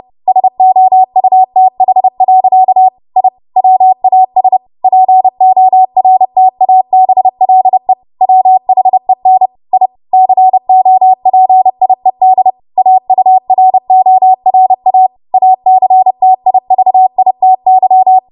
Morse code, heard as a keyed tone, about 30 wpm, 750 Hz. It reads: SOUTH. I WAS PORTABLE WHEN I COPIED AURORA ACTIVITY
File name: Morse / 30WPM